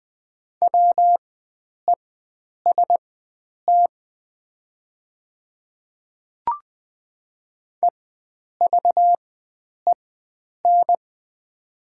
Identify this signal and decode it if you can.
Morse code, heard as a keyed tone, about 20 wpm, spaced as for 9 wpm, 700 Hz. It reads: WEST EVEN
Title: 4x Character Spacing - Top 500 Words - 20wpm